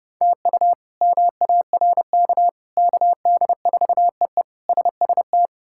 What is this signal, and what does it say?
Morse code, heard as a keyed tone, about 30 wpm, 700 Hz. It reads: TU MARK KD4EE SST
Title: K1USN SST QSO Practice - 30wpm